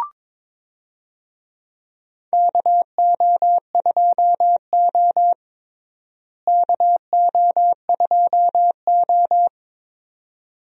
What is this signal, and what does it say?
Morse code, heard as a keyed tone, about 22 wpm, 700 Hz. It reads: KO2O KO2O
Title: USA Call Signs - 4 Characters - Repeated 2x - 22wpm